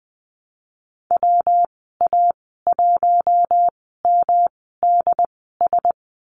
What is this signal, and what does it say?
Morse code, heard as a keyed tone, about 20 wpm, spaced as for 15 wpm, 700 Hz. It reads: WA1MDS
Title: 128 Lesson 21g - Number 6 - Call Signs 20wpm